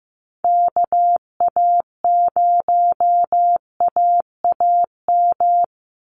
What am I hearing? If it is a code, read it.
Morse code, heard as a keyed tone, about 15 wpm, 700 Hz. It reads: KA0AAM
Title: Call Signs Repeated 3x - 15wpm